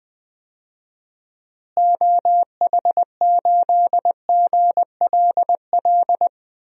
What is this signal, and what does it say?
Morse code, heard as a keyed tone, about 20 wpm, 700 Hz. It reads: OH8GLL
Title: DXCC-100 Call Signs - 6 Characters - Repeated 3x - 20wpm